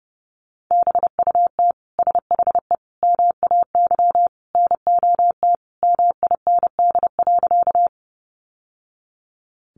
Morse code, heard as a keyed tone, about 30 wpm, 700 Hz. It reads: BUT SHE MAY NOT MIND.
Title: Sentences from Top 600 Words 30wpm